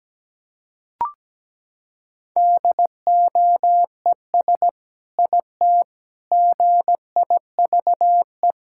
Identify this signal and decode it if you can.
Morse code, heard as a keyed tone, about 17 wpm, 700 Hz. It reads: DOES IT GIVE
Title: Sentences from Top 1600 Words 17wpm